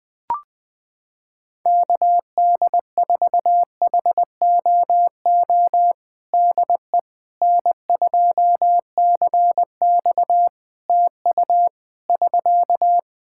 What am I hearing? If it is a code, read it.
Morse code, heard as a keyed tone, about 20 wpm, 700 Hz. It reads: KD4HOO DE N2CX TU <SK>
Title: Practice QSO - Exchange 4 - Ending - 20wpm